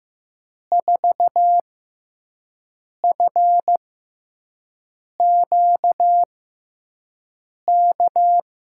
Morse code, heard as a keyed tone, about 15 wpm, spaced as for 5 wpm, 700 Hz. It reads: 4FQK
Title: USA Call Signs - 5 Characters - 6x Character Spacing - 15wpm